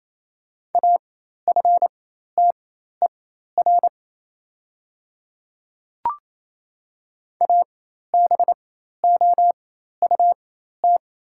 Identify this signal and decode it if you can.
Morse code, heard as a keyed tone, about 27 wpm, spaced as for 13 wpm, 690 Hz. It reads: AFTER ABOUT
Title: 4x Character Spacing - Top 500 Words - 28wpm